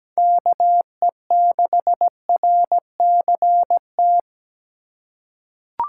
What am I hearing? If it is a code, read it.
Morse code, heard as a keyed tone, about 17 wpm, 700 Hz. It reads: KE6RCT
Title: USA Call Signs - 6 Characters - 4x Character Spacing - 17wpm